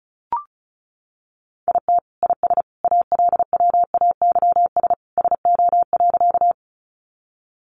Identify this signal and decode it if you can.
Morse code, heard as a keyed tone, about 34 wpm, 700 Hz. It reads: IT IS ALWAYS SO.
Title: Sentences from Top 300 Words 35wpm